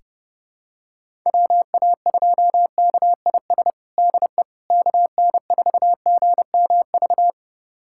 Morse code, heard as a keyed tone, about 32 wpm, 700 Hz. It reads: WA2KIS DE KN4GMV
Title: USA Callsign DE Callsign 30wpm